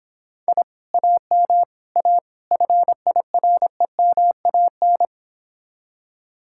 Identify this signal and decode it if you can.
Morse code, heard as a keyed tone, about 26 wpm, 700 Hz. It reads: I AM A FIREMAN